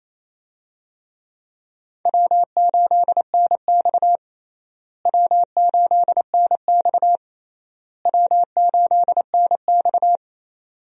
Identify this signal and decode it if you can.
Morse code, heard as a keyed tone, about 28 wpm, 700 Hz. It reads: W8NX W8NX W8NX
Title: USA Call Signs - 4 Characters - Repeated 3x - 28wpm